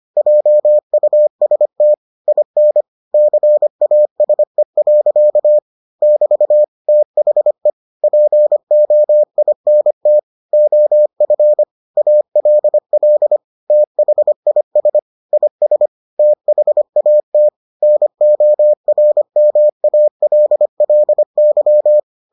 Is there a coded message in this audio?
Yes: Morse code from 0.2 to 22.0 s, heard as a keyed tone, about 25 wpm, 600 Hz. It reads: JUST IN CASE. <BT> THE POINT OF ALL THIS IS THAT NORMALLY